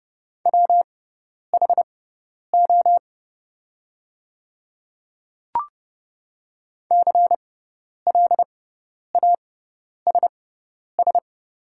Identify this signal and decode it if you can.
Morse code, heard as a keyed tone, about 30 wpm, spaced as for 10 wpm, 690 Hz. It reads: WHO CLASS
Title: 6x Character Spacing - Top 500 Words - 30wpm